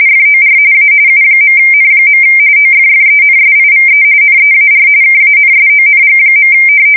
RTTY, about 45.45 baud. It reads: FOR A U.N. OFFENSIVE NEAR KANSONG.
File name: fsk170 / USS-New-Jersey